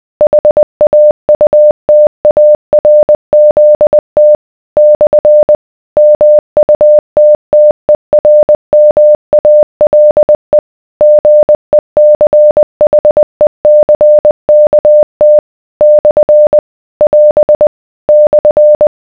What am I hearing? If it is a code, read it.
Morse code, heard as a keyed tone, about 20 wpm, 600 Hz. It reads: HAUTARZT / MUTTERMALE GECHECKT / <AS> /